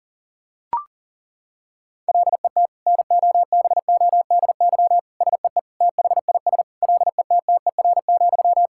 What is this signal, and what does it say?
Morse code, heard as a keyed tone, about 40 wpm, 700 Hz. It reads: LET NOBODY SEE THIS LETTER,
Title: Sentences from Top 1600 Words 40wpm